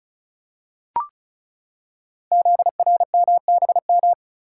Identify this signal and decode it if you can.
Morse code, heard as a keyed tone, about 36 wpm, 700 Hz. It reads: ZRMBM
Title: Code Groups - 5 Letters - 35wpm